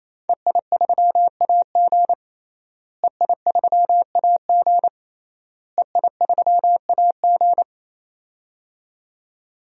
Morse code, heard as a keyed tone, about 27 wpm, 700 Hz. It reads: EI3AG EI3AG EI3AG
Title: DXCC-100 Call Signs - 5 Characters - Repeated 3x - 28wpm